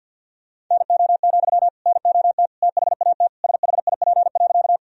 Morse code, heard as a keyed tone, about 50 wpm, 700 Hz. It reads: NO, NOT THAT SHIP.
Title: Sentences from Top 1000 Words - Review Part 2 - 50wpm